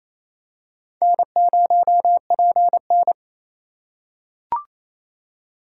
Morse code, heard as a keyed tone, about 28 wpm, 700 Hz. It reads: N0PN
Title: USA Call Signs - 4 Characters - 3x Character Spacing - 28wpm